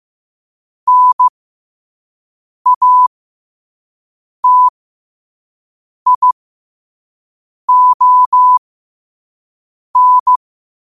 Morse code, heard as a keyed tone, about 15 wpm, spaced as for 5 wpm, 1000 Hz. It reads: NATION
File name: qotdfiles / 5wpm